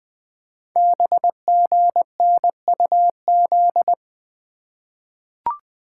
Morse code, heard as a keyed tone, about 20 wpm, 700 Hz. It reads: BGNUZ
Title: Code Groups - 5 Letters - 20wpm